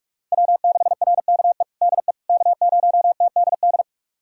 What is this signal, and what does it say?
Morse code, heard as a keyed tone, about 46 wpm, 700 Hz. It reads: W6RKE DE K0TDD
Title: USA Callsign DE Callsign - 5 Characters - 45wpm